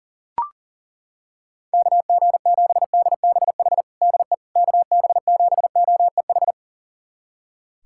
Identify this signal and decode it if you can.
Morse code, heard as a keyed tone, about 40 wpm, 690 Hz. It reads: KG7DBH DE KB7OEH
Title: USA Callsign DE Callsign - 6 Characters - 40wpm